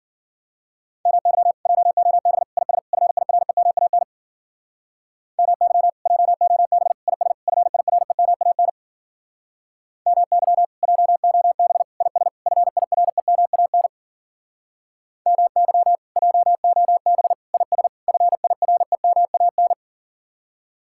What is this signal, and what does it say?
Morse code, heard as a keyed tone, about 50 wpm, 700 Hz. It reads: MY JOB IS FIREMAN MY JOB IS FIREMAN MY JOB IS FIREMAN MY JOB IS FIREMAN